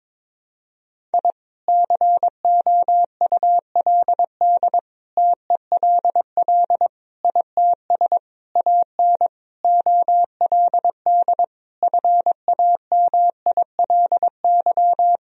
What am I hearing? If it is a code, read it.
Morse code, heard as a keyed tone, about 22 wpm, 700 Hz. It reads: I COULD TELL ITS AN OLD FAMILY
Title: Sentences from Top 1000 Words - Review Part 2 - 22wpm